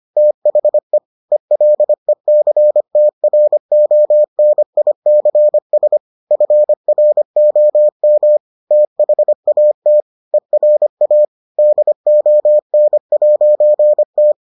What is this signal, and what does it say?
Morse code, heard as a keyed tone, about 25 wpm, 600 Hz. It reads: THE ELECTRONICS FROM THAT ERA DON'T